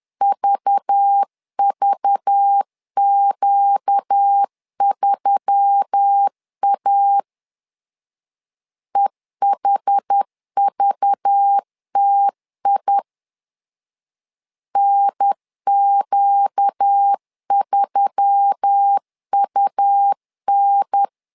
Morse code, heard as a keyed tone, about 10 wpm, 800 Hz. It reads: VVQ3A EHVTI NQ3UN
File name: LEZIONI / 5 LBVC / 5 LBV 10wpm 2248